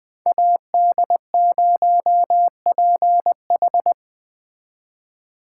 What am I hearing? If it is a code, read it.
Morse code, heard as a keyed tone, about 20 wpm, 700 Hz. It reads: AD0PH